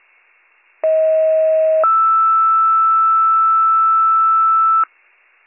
Two-tone page, heard as a one-second tone, then a three-second tone, about 640 then 1320 Hz. It is heard from 0.8 to 4.8 s.